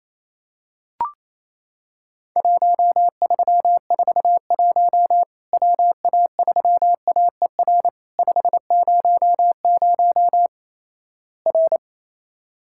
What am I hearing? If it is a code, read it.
Morse code, heard as a keyed tone, about 27 wpm, 700 Hz. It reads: 1341 WA3AER 500 R
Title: Worked All Europe DX Contest - QTC Exchange - 28wpm